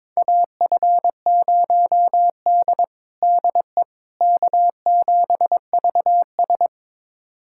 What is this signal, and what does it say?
Morse code, heard as a keyed tone, about 22 wpm, 700 Hz. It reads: AF0D DE K7VS
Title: USA Callsign DE Callsign - 4 Characters - 22wpm